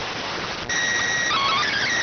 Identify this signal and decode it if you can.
SSTV (slow-scan television), PD 290.